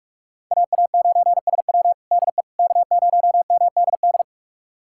Morse code, heard as a keyed tone, about 47 wpm, 700 Hz. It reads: AA9SW DE K0MDD
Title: USA Callsign DE Callsign 45wpm